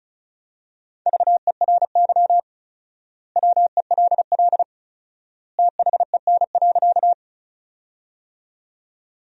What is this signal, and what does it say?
Morse code, heard as a keyed tone, about 36 wpm, 700 Hz. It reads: VERY WELL THEN.
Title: 4x Word-Spacing - Sentences from Top 200 Words - 35wpm